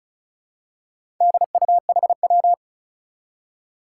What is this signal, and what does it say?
Morse code, heard as a keyed tone, about 36 wpm, 700 Hz. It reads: DUHW
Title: Four Challenging Letter-Number Combinations 35wpm